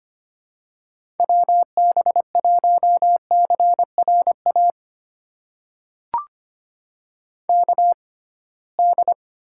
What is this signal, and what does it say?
Morse code, heard as a keyed tone, about 25 wpm, spaced as for 19 wpm, 700 Hz. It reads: WB1CRA KD
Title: USA Call Signs - 6 Characters - 6x Character Spacing - 25wpm